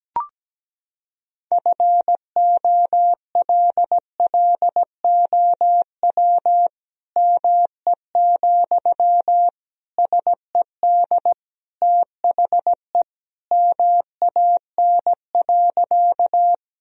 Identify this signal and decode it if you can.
Morse code, heard as a keyed tone, about 17 wpm, 700 Hz. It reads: FOLLOW ME, SED THE MAN.
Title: Sentences from Top 200 Words and Abbreviations 17wpm